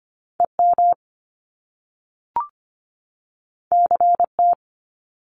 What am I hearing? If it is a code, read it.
Morse code, heard as a keyed tone, about 25 wpm, 700 Hz. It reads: EM CT
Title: Top Two Letter Combinations in Words 25wpm